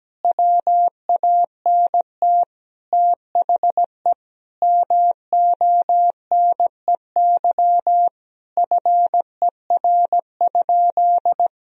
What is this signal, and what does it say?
Morse code, heard as a keyed tone, about 17 wpm, 700 Hz. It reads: WANT THE MONEY FER?
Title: Sentences from Top 400 Words and Abbreviations 17wpm